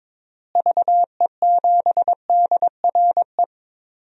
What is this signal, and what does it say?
Morse code, heard as a keyed tone, about 21 wpm, 700 Hz. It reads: VE7DRE
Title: DXCC-100 Call Signs - 6 Characters - Repeated 2x - 22wpm